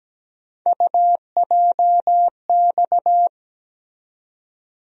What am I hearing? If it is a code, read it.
Morse code, heard as a keyed tone, about 17 wpm, 700 Hz. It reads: UJX
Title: USA Call Sign Postfixes - 3 Letters 17wpm